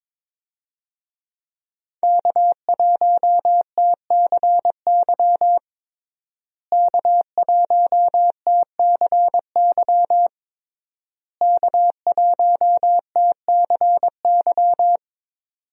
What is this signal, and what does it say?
Morse code, heard as a keyed tone, about 22 wpm, 700 Hz. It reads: K1TCY K1TCY K1TCY
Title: USA Call Signs - 5 Characters - Repeated 3x - 22wpm